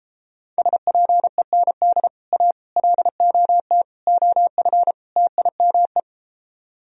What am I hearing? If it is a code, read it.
Morse code, heard as a keyed tone, about 33 wpm, 700 Hz. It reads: SPEND A LOT OF TIME